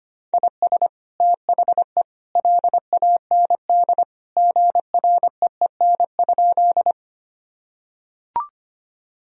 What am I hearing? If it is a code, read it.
Morse code, heard as a keyed tone, about 25 wpm, 700 Hz. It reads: IS THE LAND GREEN?
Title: Sentences from Top 1000 Words - Review Part 2 - 25wpm